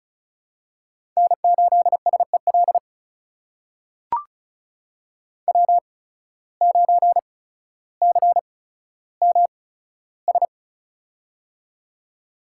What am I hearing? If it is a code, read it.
Morse code, heard as a keyed tone, about 36 wpm, spaced as for 15 wpm, 700 Hz. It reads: N8SEL W9CMS
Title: USA Call Signs - 5 Characters - 8x Character Spacing - 35wpm